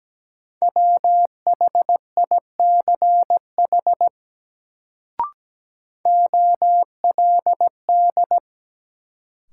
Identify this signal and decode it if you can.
Morse code, heard as a keyed tone, about 17 wpm, 700 Hz. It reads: WHICH OLD
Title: Sets of 2 Words - New Word Drill - 17wpm